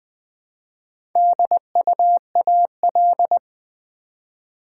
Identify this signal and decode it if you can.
Morse code, heard as a keyed tone, about 20 wpm, 700 Hz. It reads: DUAL
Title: Build A Word - 2 Letters At A Time - 4-Letter Words - 20wpm